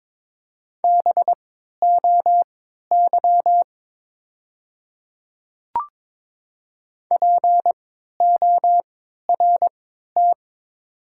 Morse code, heard as a keyed tone, about 22 wpm, spaced as for 12 wpm, 700 Hz. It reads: BOY PORT
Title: 3x Character Spacing - Top 200 Words - 22wpm